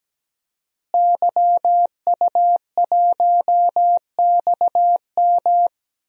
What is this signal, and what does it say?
Morse code, heard as a keyed tone, about 17 wpm, 700 Hz. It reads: YU1XM
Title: DXCC-100 Call Signs - 5 Characters - Repeated 3x - 17wpm